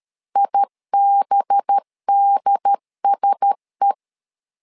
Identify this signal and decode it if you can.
Morse code, heard as a keyed tone, about 13 wpm, 800 Hz. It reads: IBDSE
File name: LEZIONI / 5 LBVC / 5 LBVC 12wpm 1955